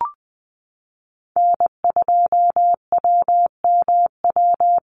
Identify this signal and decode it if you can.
Morse code, heard as a keyed tone, about 19 wpm, 700 Hz. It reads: N2WMW
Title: USA Call Signs Repeated 2x 20wpm